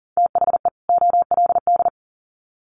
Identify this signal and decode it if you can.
Morse code, heard as a keyed tone, about 40 wpm, 700 Hz. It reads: THE OLD
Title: Sets of 2 Words 40wpm